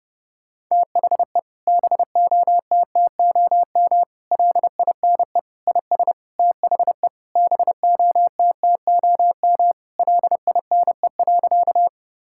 Morse code, heard as a keyed tone, about 30 wpm, 700 Hz. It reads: THE BOTTOM LINE IS THE BOTTOM LINE.
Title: Sentences from Top 1000 Words - Review Part 6 - 30wpm